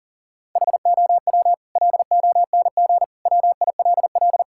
Morse code, heard as a keyed tone, about 37 wpm, 700 Hz. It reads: HOW LONG WILL
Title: Sentences from Top 1000 Words - Review Part 6 - 40wpm